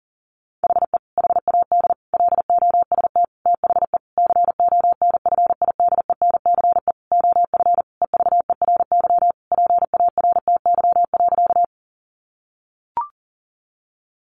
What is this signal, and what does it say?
Morse code, heard as a keyed tone, about 37 wpm, 700 Hz. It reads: HE HAD LOST THE CONFIDENCE OF EVERY PARTY.
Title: Sentences from Top 1500 Words 40wpm